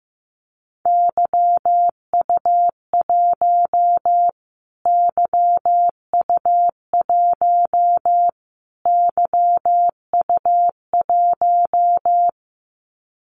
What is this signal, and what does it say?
Morse code, heard as a keyed tone, about 15 wpm, 700 Hz. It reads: YU1 YU1 YU1
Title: Callsign Prefixes for Top 25 DXCC - Repeated 3x - 15wpm